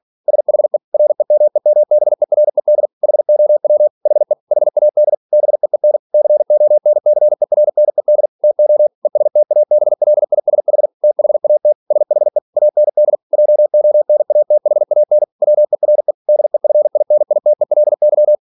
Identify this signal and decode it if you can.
Morse code, heard as a keyed tone, about 46 wpm, 600 Hz. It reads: SHE REMEMBERED HOW HE HAD BEEN CONCERNED TO ESTABLISH THAT SHE AND JONATHAN WERE DEFINITELY
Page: Never 47wpm